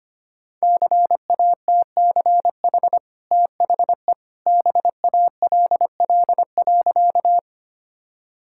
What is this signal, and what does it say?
Morse code, heard as a keyed tone, about 25 wpm, 690 Hz. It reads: CATCH THE BALL.
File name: Easy Sentences with Hints - 25wpm